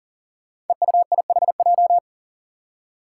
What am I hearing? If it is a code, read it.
Morse code, heard as a keyed tone, about 38 wpm, 700 Hz. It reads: EUIHJ
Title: Five Challenging Letter-Number Combinations 40wpm